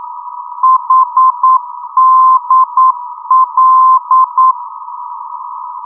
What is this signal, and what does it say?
Morse code, heard as a keyed tone, about 9 wpm, 1040 Hz. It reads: HDL
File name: europe / d-8060-cw-id-2004-rb